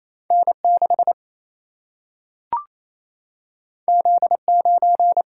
Morse code, heard as a keyed tone, about 28 wpm, 700 Hz. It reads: N 6 Z 9
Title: Single-Letter Single-Number 28wpm